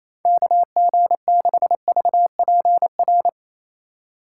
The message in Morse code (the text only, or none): KG6VPR